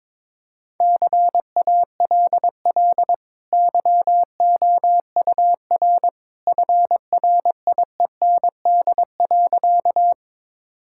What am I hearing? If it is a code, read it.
Morse code, heard as a keyed tone, about 22 wpm, 700 Hz. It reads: CALL YOUR FRIEND.
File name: practice / 22wpm / Easy Sentences with Hints - 22wpm